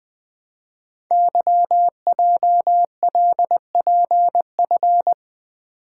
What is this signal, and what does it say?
Morse code, heard as a keyed tone, about 20 wpm, 700 Hz. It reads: YJLPF
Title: Code Groups - 5 Letters - 20wpm